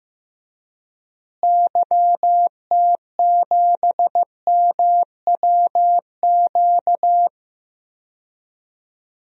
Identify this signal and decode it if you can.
Morse code, heard as a keyed tone, about 15 wpm, 700 Hz. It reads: YT7MWQ
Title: DXCC-100 Call Signs - 6 Characters - Repeated 3x - 15wpm